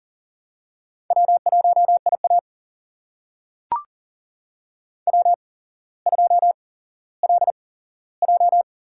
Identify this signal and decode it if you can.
Morse code, heard as a keyed tone, about 40 wpm, spaced as for 17 wpm, 700 Hz. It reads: W1IA W2LJ
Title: USA Call Signs - 4 Characters - 8x Character Spacing - 40wpm